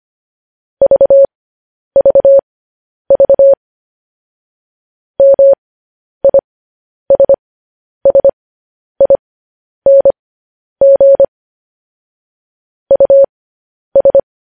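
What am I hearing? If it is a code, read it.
Morse code, heard as a keyed tone, about 25 wpm, spaced as for 10 wpm, 550 Hz. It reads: VVV MISSING US
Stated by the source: Morse news headlines sent at 10 wpm.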